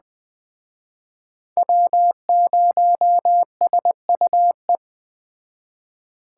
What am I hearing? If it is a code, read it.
Morse code, heard as a keyed tone, about 20 wpm, 700 Hz. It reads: W0SUE
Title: Speed Racing - USA 1x3 Call Signs 17wpm